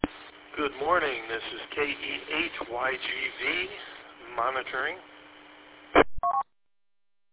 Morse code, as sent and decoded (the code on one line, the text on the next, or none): none